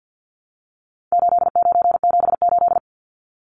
Morse code, heard as a keyed tone, about 50 wpm, 700 Hz. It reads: 8978
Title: Numbers - 4-Digit 50wpm